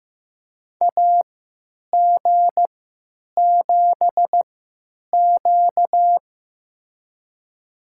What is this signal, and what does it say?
Morse code, heard as a keyed tone, about 15 wpm, spaced as for 9 wpm, 700 Hz. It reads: AG7Q
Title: USA Call Signs - 4 Characters - 3x Character Spacing - 15wpm